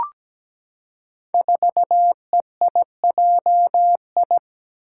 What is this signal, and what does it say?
Morse code, heard as a keyed tone, about 17 wpm, 700 Hz. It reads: 4EIJI
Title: Five Challenging Letter-Number Combinations 17wpm